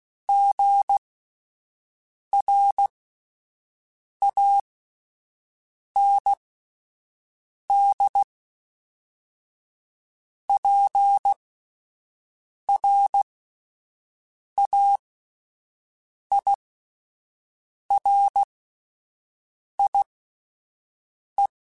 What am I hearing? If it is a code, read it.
Morse code, heard as a keyed tone, about 16 wpm, spaced as for 5 wpm, 800 Hz. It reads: GRANDPRAIRIE
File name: fromSMs / FromSMs mp3s / Cities@5